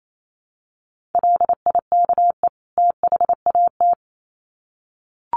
Morse code, heard as a keyed tone, about 28 wpm, 700 Hz. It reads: LIKE THAT
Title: Sets of 2 Words 28wpm